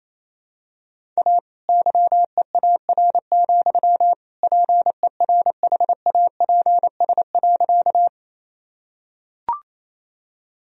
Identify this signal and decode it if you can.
Morse code, heard as a keyed tone, about 28 wpm, 700 Hz. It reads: A YEAR, PERHAPS.
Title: Sentences from Top 1000 Words - Review Part 1 - 28wpm